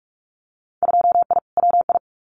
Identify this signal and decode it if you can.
Morse code, heard as a keyed tone, about 46 wpm, 700 Hz. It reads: 2I WI
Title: Winter Field Day Exchanges 45wpm